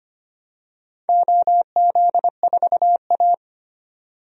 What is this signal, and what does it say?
Morse code, heard as a keyed tone, about 25 wpm, 700 Hz. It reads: OZ4A
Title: DXCC-100 Call Signs - 4 Characters - 25wpm